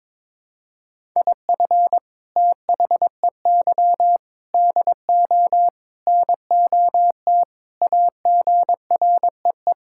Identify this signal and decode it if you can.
Morse code, heard as a keyed tone, about 22 wpm, 700 Hz. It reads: IF THEY DO NOT AGREE
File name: practice / 22wpm / Sentences with 4-Word Phrases - 22wpm